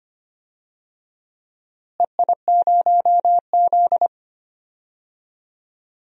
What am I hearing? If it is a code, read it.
Morse code, heard as a keyed tone, about 25 wpm, 700 Hz. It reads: EI0Z
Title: DXCC-100 Call Signs - 4 Characters - Repeated 2x - 25wpm